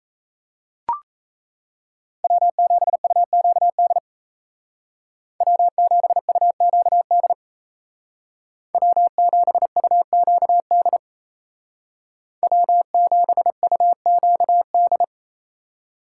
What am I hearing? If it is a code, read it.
Morse code, heard as a keyed tone, about 37 wpm, 700 Hz. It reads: W7UQD W7UQD W7UQD W7UQD